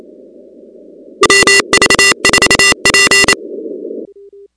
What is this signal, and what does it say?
Morse code, heard as a keyed tone, about 27 wpm, 2000 Hz, with noise. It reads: MV4P
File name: contests / 2019-cw-wpx-kt5j-k5pi- -2330 0-13